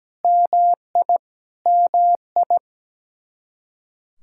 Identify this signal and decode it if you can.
Morse code, heard as a keyed tone, about 17 wpm, 700 Hz. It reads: MI MI
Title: Practice QSO - Exchange 1 - Beginning - 17wpm